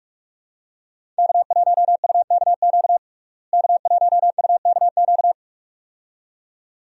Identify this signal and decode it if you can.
Morse code, heard as a keyed tone, about 46 wpm, 700 Hz. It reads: K1UKQ K1UKQ
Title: USA Call Signs - 5 Characters - Repeated 2x - 45wpm